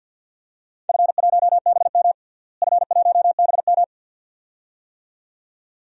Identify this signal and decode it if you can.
Morse code, heard as a keyed tone, about 46 wpm, 700 Hz. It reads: F1BM F1BM
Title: DXCC-100 Call Signs - 4 Characters - Repeated 2x - 50wpm